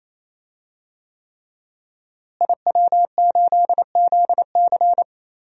Morse code, heard as a keyed tone, about 28 wpm, 700 Hz. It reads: IW8ZC
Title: DXCC-100 Call Signs - 5 Characters - Repeated 3x - 28wpm